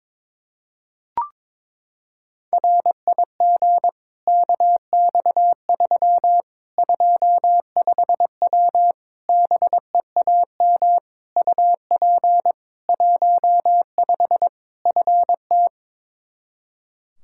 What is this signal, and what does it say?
Morse code, heard as a keyed tone, about 22 wpm, 700 Hz. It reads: RIG KX3 25W BEAM UP 15 FT
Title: K1USN SST QSO Practice 22wpm